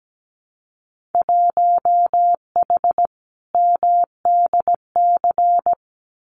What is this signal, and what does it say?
Morse code, heard as a keyed tone, about 17 wpm, 700 Hz. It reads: 1H MDC
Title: Winter Field Day Exchanges 17wpm